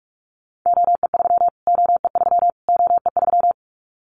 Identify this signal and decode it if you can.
Morse code, heard as a keyed tone, about 44 wpm, 700 Hz. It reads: OE3 OE3 OE3
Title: Callsign Prefixes for Top 25 DXCC - Repeated 3x - 45wpm